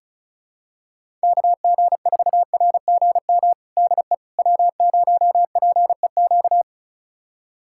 Morse code, heard as a keyed tone, about 36 wpm, 700 Hz. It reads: KG4RGM DE W0PEQ
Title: USA Callsign DE Callsign 35wpm